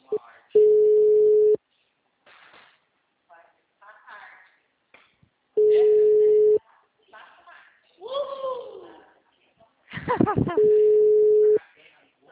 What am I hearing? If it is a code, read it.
Ringback tone (Europe, 425 Hz), with speech.
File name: out- -2022- - - .74763